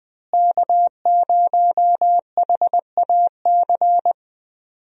Morse code, heard as a keyed tone, about 20 wpm, 700 Hz. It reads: K0HAC
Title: USA Call Signs - 5 Characters - Repeated 3x - 20wpm